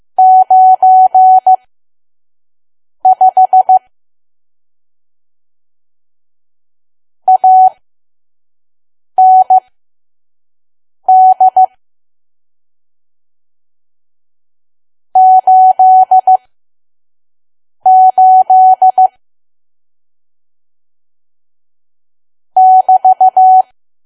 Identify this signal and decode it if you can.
Morse code, heard as a keyed tone, about 15 wpm, spaced as for 5 wpm, 750 Hz. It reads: 95 AND 88 <BT>
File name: Archive / 5 WPM files / PM